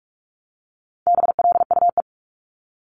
Morse code, heard as a keyed tone, about 46 wpm, 700 Hz. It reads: BLUE
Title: Multi-letter Phonemes Repeated 3x 45wpm